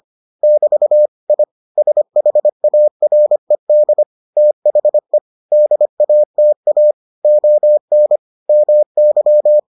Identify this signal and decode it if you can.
Morse code, heard as a keyed tone, about 25 wpm, 600 Hz. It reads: <BT> I SHARED THE DATA ON MY